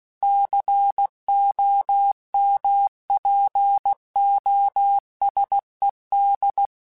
Morse code, heard as a keyed tone, about 16 wpm, 800 Hz. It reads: COMPOSED
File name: Technical Subjects / Technical Subjects-mp3s / Sw Pwr Sup @15